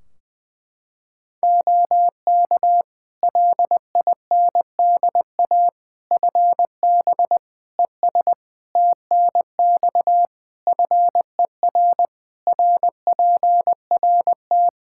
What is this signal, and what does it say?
Morse code, heard as a keyed tone, about 21 wpm, 700 Hz. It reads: OK LINDA FB ES TNX FER RPRT
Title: Practice QSO - Exchange 2 - Info - 20wpm